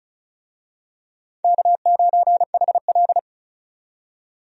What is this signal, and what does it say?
Morse code, heard as a keyed tone, about 36 wpm, 700 Hz. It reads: K9HL
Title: USA Call Signs - 4 Characters - 35wpm